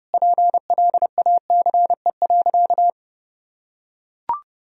Morse code, heard as a keyed tone, about 32 wpm, 700 Hz. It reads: PLACE.
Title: Sentences from Top 1000 Words - Review Part 1 - 30wpm